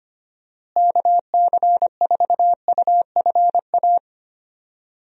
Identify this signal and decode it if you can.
Morse code, heard as a keyed tone, about 25 wpm, 700 Hz. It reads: KC4UFA
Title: USA Call Signs - 6 Characters - Repeated 2x - 25wpm